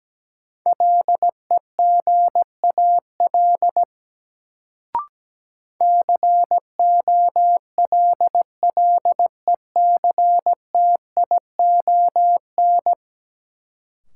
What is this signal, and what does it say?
Morse code, heard as a keyed tone, about 17 wpm, 700 Hz. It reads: LEGAL COLLECTION
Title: Sentences from Top 1200 Words - New Word Drill Alt - 17wpm